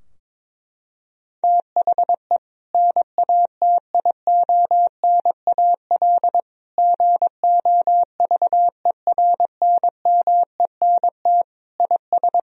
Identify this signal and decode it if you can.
Morse code, heard as a keyed tone, about 22 wpm, 700 Hz. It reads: THE NATIONAL GOVERNMENT IS
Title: Sentences from Top 1000 Words - Review Part 4 - 22wpm